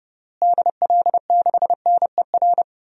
Morse code, heard as a keyed tone, about 30 wpm, 700 Hz. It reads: DL6NER
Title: DXCC-100 Call Signs - 6 Characters - Repeated 3x - 30wpm